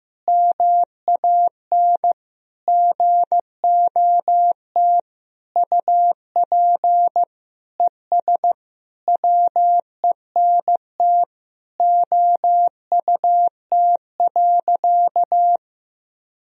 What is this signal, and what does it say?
Morse code, heard as a keyed tone, about 15 wpm, 700 Hz. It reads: MAN GOT UP ES WENT OUT.
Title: Sentences from Top 300 Words and Abbreviations 15wpm